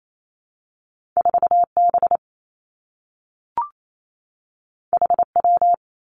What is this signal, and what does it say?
Morse code, heard as a keyed tone, about 28 wpm, 700 Hz. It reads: 4 B H W
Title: Two Challenging Letter-Number Combinations 28wpm